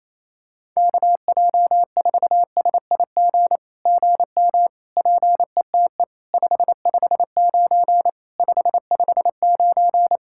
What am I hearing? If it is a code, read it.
Morse code, heard as a keyed tone, about 28 wpm, 700 Hz. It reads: KJ4SIG GM PETE 559 559
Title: SOTA QSO Practice 28wpm